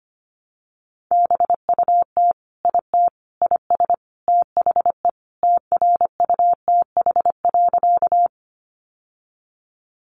Morse code, heard as a keyed tone, about 25 wpm, 700 Hz. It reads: BUT IT IS THE TRUTH.
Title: Sentences from Top 800 Words 25wpm